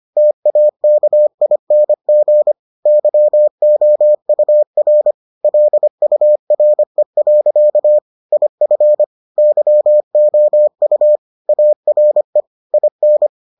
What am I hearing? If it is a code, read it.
Morse code, heard as a keyed tone, about 25 wpm, 600 Hz. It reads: TAKING YOUR LURE. IF YOU ARE IN